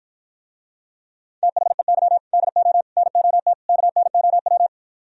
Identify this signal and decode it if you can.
Morse code, heard as a keyed tone, about 57 wpm, 700 Hz. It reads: THEY DO NOT KNOW